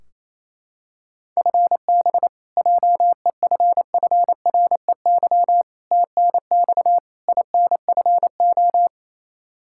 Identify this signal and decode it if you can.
Morse code, heard as a keyed tone, about 28 wpm, 700 Hz. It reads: FB JEFFREY TNX INFO
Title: K1USN SST QSO Practice 28wpm